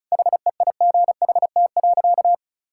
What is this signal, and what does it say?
Morse code, heard as a keyed tone, about 34 wpm, 700 Hz. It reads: HEIGHT.
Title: Sentences from Top 1600 Words 35wpm